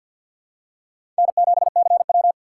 Morse code, heard as a keyed tone, about 51 wpm, 700 Hz. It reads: N7CW
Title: K1USN SST QSO Practice - 50wpm